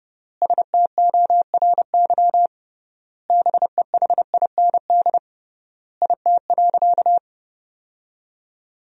Morse code, heard as a keyed tone, about 30 wpm, 700 Hz. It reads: STORY BEHIND IT.
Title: 3x Word-Spacing - Sentences from Top 500 Words - 30wpm